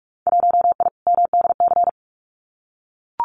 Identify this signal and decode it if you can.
Morse code, heard as a keyed tone, about 44 wpm, 700 Hz. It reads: 1I MDC
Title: Winter Field Day Exchanges 45wpm